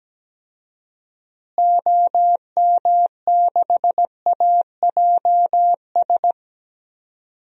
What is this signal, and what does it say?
Morse code, heard as a keyed tone, about 17 wpm, 700 Hz. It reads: OM6AJS
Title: DXCC-100 Call Signs - 6 Characters - Repeated 2x - 17wpm